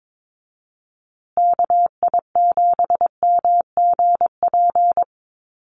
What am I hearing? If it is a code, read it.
Morse code, heard as a keyed tone, about 22 wpm, 700 Hz. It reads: KI7MGP
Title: USA Call Signs Repeated 2x 22wpm